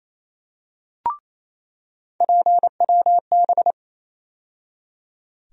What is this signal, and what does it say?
Morse code, heard as a keyed tone, about 28 wpm, 700 Hz. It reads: PWB
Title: USA Call Sign Postfixes - 3 Letters 28wpm